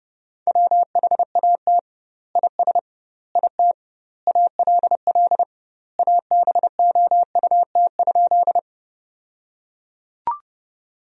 Morse code, heard as a keyed tone, about 32 wpm, 700 Hz. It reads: WHAT IS IT ALL ABOUT?
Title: 2x Word-Spacing - Sentences from Top 100 Words - 30wpm